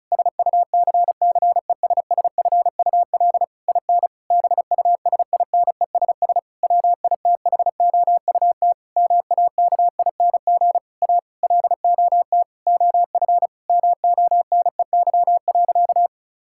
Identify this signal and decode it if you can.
Morse code, heard as a keyed tone, about 34 wpm, 700 Hz. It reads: SUCCESSFUL IN BUSINESS WITHOUT MAKING A LOT OF MONEY.
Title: Sentences from Top 1000 Words - Review Part 6 - 35wpm